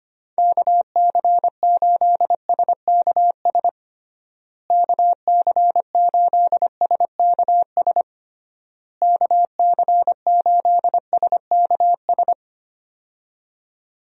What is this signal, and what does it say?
Morse code, heard as a keyed tone, about 25 wpm, 700 Hz. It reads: KC8SKS KC8SKS KC8SKS
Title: USA Call Signs - 6 Characters - Repeated 3x - 25wpm